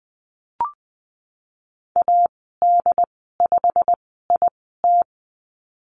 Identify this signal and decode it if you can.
Morse code, heard as a keyed tone, about 20 wpm, spaced as for 15 wpm, 700 Hz. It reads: AD5IT
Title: 147 Lesson 24g - Letter y - Call Signs 20wpm